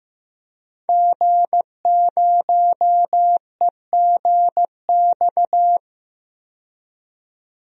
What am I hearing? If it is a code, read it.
Morse code, heard as a keyed tone, about 15 wpm, 700 Hz. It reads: G0EGX
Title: DXCC-100 Call Signs - 5 Characters - 15wpm